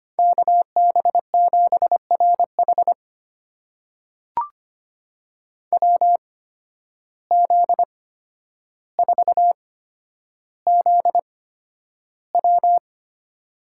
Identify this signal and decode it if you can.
Morse code, heard as a keyed tone, about 25 wpm, spaced as for 11 wpm, 700 Hz. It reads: KB7RH WZ4ZW
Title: USA Call Signs - 5 Characters - 8x Character Spacing - 25wpm